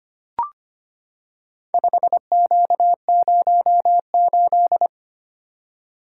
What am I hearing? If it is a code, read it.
Morse code, heard as a keyed tone, about 25 wpm, 700 Hz. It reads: 5Q08
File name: practice / 25wpm / Four Letter-Number Combinations - 25wpm